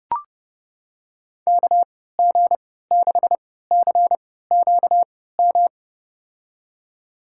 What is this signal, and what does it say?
Morse code, heard as a keyed tone, about 31 wpm, spaced as for 17 wpm, 700 Hz. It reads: KG6CQM
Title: USA Call Signs - 3x Character Spacing - 30wpm